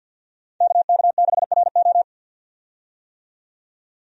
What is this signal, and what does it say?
Morse code, heard as a keyed tone, about 49 wpm, 700 Hz. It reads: KK6RO
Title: USA Call Signs - 5 Characters - 50wpm